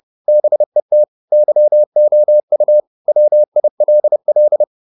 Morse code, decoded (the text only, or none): BET YOU WILL